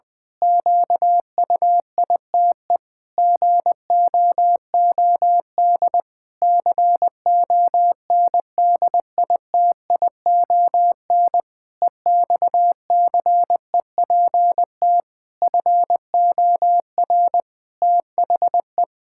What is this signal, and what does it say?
Morse code, heard as a keyed tone, about 21 wpm, 700 Hz. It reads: QUITE GOOD CONDITION EXCEPT FOR THE
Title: Pangram Challenge v2 - 20wpm